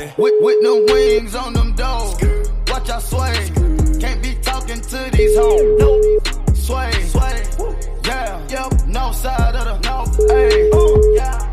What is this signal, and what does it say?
Ringback tone (Europe, 425 Hz), with noise.